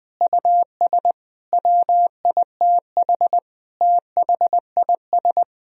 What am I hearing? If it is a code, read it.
Morse code, heard as a keyed tone, about 21 wpm, 700 Hz. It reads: US WITH THIS
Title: Sentences from Top 1000 Words - Review Part 2 - 20wpm